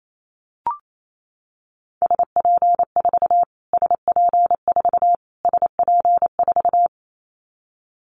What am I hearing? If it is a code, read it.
Morse code, heard as a keyed tone, about 28 wpm, 700 Hz. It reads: SP4 SP4 SP4
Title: Callsign Prefixes for Top 25 DXCC - Repeated 3x - 28wpm